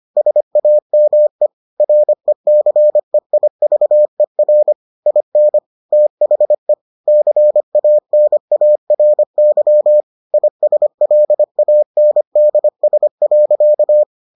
Morse code, heard as a keyed tone, about 25 wpm, 600 Hz. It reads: SAME RECEIVER IN THE CANARY ISLANDS.